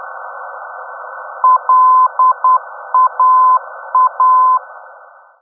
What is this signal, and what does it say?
Morse code, heard as a keyed tone, about 10 wpm, 1020 Hz. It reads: LAA
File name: europe / d-5078- dec 03 - ndb laa-ti